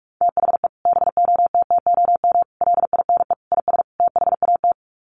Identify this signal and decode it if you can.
Morse code, heard as a keyed tone, about 44 wpm, 700 Hz. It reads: THE BOTTOM LINE IS THAT
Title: Sets of 5 Words Encore 45wpm